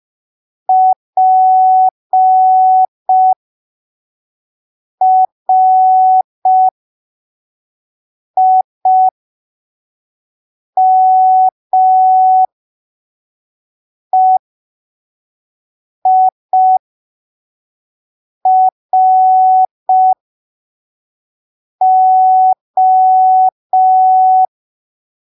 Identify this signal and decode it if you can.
Morse code, heard as a keyed tone, about 5 wpm, spaced as for 3 wpm, 750 Hz. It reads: PRIMEIRO